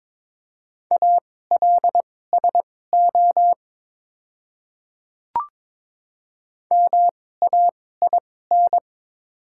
Morse code, heard as a keyed tone, about 22 wpm, spaced as for 16 wpm, 690 Hz. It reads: ALSO MAIN
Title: 2x Character Spacing - Top 300 Words - 22wpm